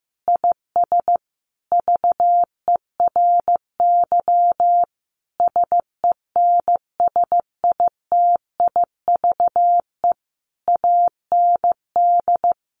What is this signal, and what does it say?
Morse code, heard as a keyed tone, about 15 wpm, 700 Hz. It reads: IS VERY SENSITIVE AND
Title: Sentences from Top 1400 Words 15wpm